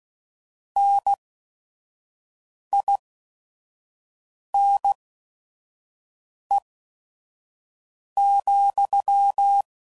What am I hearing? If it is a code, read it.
Morse code, heard as a keyed tone, about 16 wpm, spaced as for 5 wpm, 800 Hz. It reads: NINE,